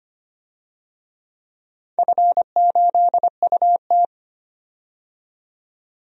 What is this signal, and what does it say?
Morse code, heard as a keyed tone, about 25 wpm, 700 Hz. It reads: F8UT
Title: DXCC-100 Call Signs - 4 Characters - 25wpm